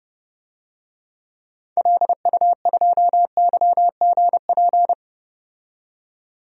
Morse code, heard as a keyed tone, about 30 wpm, 700 Hz. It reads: LU2YGP
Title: DXCC-100 Call Signs - 6 Characters - 30wpm